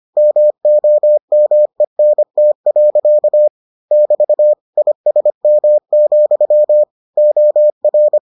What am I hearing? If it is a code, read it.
Morse code, heard as a keyed tone, about 25 wpm, 600 Hz. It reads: MOMENT. <BT> ISM, OR